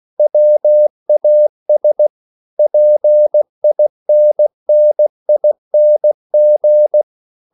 Morse code, heard as a keyed tone, about 16 wpm, 600 Hz. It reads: WAS PINNING